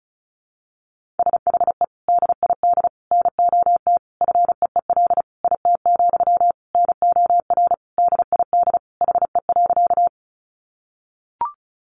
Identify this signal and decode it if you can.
Morse code, heard as a keyed tone, about 34 wpm, 700 Hz. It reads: SHE DID NOT FEEL IT, NOR DID HE.
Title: Sentences from Top 1500 Words 35wpm